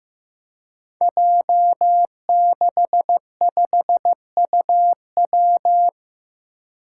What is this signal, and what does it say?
Morse code, heard as a keyed tone, about 15 wpm, 700 Hz. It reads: J65UW
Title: Five Challenging Letter-Number Combinations 15wpm